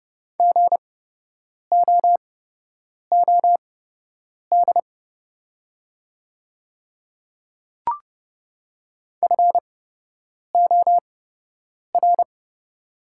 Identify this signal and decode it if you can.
Morse code, heard as a keyed tone, about 31 wpm, spaced as for 8 wpm, 700 Hz. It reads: GOOD FOR
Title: 8x Character Spacing - Top 200 Words - 30wpm